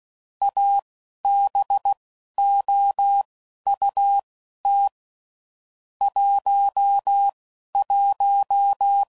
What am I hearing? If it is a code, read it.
Morse code, heard as a keyed tone, about 16 wpm, spaced as for 11 wpm, 800 Hz. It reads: ABOUT 11
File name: CW Code Groups @10WPM / QSOs / QSO WA7QOX